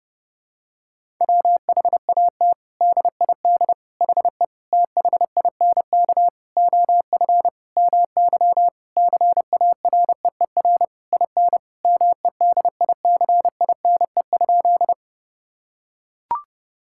Morse code, heard as a keyed tone, about 32 wpm, 700 Hz. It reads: WHAT DID HE THINK OF MY CAREER IN MEDICINE?
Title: Sentences from Top 1000 Words - Review Part 6 - 30wpm